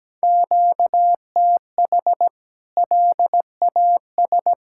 Morse code, heard as a keyed tone, about 17 wpm, 700 Hz. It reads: QTH LAS
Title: K1USN SST QSO Practice 17wpm